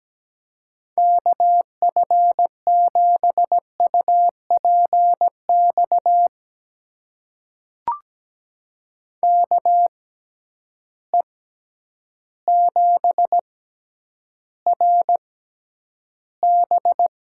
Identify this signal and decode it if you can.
Morse code, heard as a keyed tone, about 17 wpm, spaced as for 9 wpm, 700 Hz. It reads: KF7UPX KE7RB
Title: USA Call Signs - 6 Characters - 6x Character Spacing - 17wpm